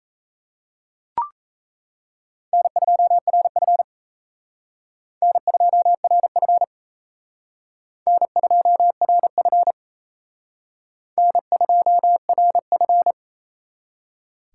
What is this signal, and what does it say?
Morse code, heard as a keyed tone, about 35 wpm, 700 Hz. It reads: N2RF N2RF N2RF N2RF
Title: Speed Racing - USA 1x2 Call Signs 28wpm